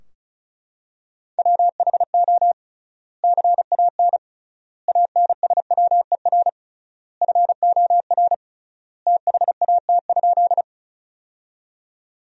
Morse code, heard as a keyed tone, about 36 wpm, 700 Hz. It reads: WHO CAN ANSWER FOR THAT?
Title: 3x Word-Spacing - Sentences from Top 300 Words - 35wpm